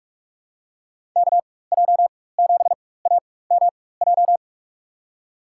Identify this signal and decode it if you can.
Morse code, heard as a keyed tone, about 46 wpm, spaced as for 21 wpm, 700 Hz. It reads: KJ7AMJ
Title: USA Call Signs - 6 Characters - 4x Character Spacing - 45wpm